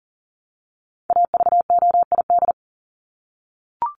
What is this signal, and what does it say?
Morse code, heard as a keyed tone, about 40 wpm, 700 Hz. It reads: AVOID
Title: Top 751-1000 Words 40wpm